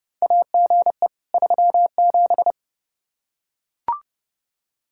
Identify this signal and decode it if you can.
Morse code, heard as a keyed tone, about 30 wpm, 700 Hz. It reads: AGE 37
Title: Practice QSO - Exchange 3 - More Info - 30wpm